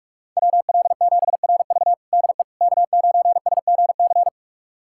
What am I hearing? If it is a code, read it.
Morse code, heard as a keyed tone, about 46 wpm, 700 Hz. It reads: WL7RV DE K9SGC
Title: USA Callsign DE Callsign - 5 Characters - 45wpm